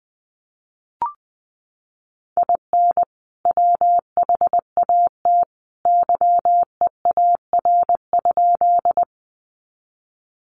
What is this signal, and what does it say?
Morse code, heard as a keyed tone, about 20 wpm, 700 Hz. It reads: IN WHAT YEAR?
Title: Sentences from Top 200 Words 20wpm